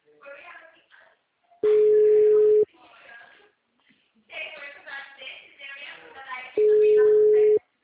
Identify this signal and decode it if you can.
Ringback tone (Europe, 425 Hz), with speech.